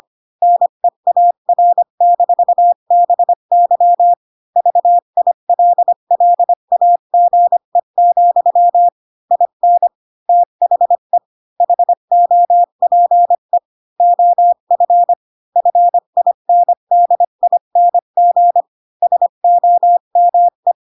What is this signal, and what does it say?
Morse code, heard as a keyed tone, about 25 wpm, 700 Hz. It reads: NEAR-BY VILLAGE, IN THE HOPE OF FINDING SOME
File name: Pinocchio / Pinocchio ch5 20 25 ik1hge